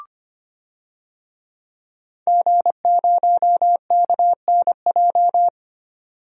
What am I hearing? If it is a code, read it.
Morse code, heard as a keyed tone, about 25 wpm, 700 Hz. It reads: G0KNJ
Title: DXCC-100 Call Signs - 5 Characters - 25wpm